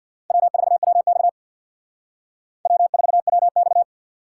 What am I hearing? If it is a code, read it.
Morse code, heard as a keyed tone, about 61 wpm, 700 Hz. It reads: W4WX W4WX
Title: Speed Racing - USA 1x2 Call Signs 50wpm